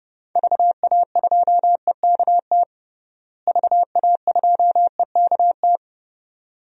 Morse code, heard as a keyed tone, about 32 wpm, 700 Hz. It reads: VA2EKT VA2EKT
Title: DXCC-100 Call Signs - 6 Characters - Repeated 2x - 30wpm